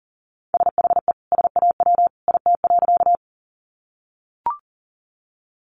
Morse code, heard as a keyed tone, about 40 wpm, 700 Hz. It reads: SHE SAW IT.
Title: Sentences from Top 300 Words 40wpm